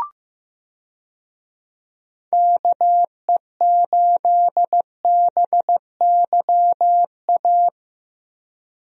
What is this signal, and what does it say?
Morse code, heard as a keyed tone, about 15 wpm, 700 Hz. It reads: KE8BYA
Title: USA Call Signs - 6 Characters - 15wpm